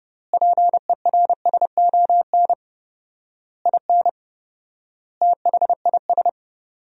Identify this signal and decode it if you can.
Morse code, heard as a keyed tone, about 30 wpm, 700 Hz. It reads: PERSON IN THIS
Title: 4x Word-Spacing - Sentences from Top 400 Words - 30wpm